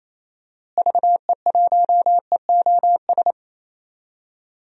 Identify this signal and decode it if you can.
Morse code, heard as a keyed tone, about 28 wpm, 700 Hz. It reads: VE1EOS
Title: DXCC-100 Call Signs - 6 Characters - Repeated 3x - 28wpm